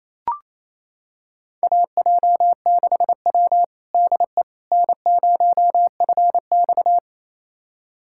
Morse code, heard as a keyed tone, about 27 wpm, 690 Hz. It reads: AJ6W DE N0FX
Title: USA Callsign DE Callsign - 4 Characters - 28wpm